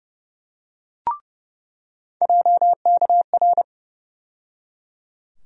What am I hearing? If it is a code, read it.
Morse code, heard as a keyed tone, about 30 wpm, 700 Hz. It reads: JKR
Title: USA Call Sign Postfixes - 3 Letters 30wpm